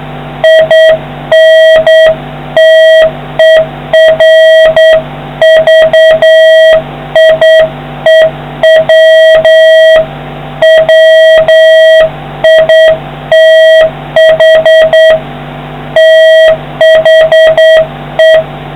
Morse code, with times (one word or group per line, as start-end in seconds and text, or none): 0.4-10.0 INTERVIEW
10.6-15.1 WITH
16.0-18.3 THE